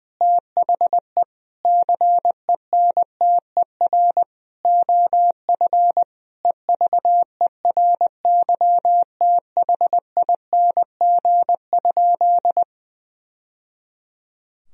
Morse code, heard as a keyed tone, about 19 wpm, 700 Hz. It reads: THE CENTER OF EVERYTHING?
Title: Sentences from Top 1000 Words - Review Part 2 - 20wpm